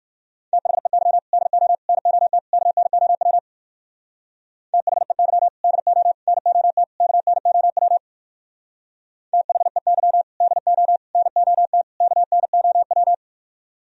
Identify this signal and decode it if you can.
Morse code, heard as a keyed tone, about 51 wpm, 700 Hz. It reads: THEY DO NOT KNOW THEY DO NOT KNOW THEY DO NOT KNOW
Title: Speed-Racing - Sets of 4 Words 45wpm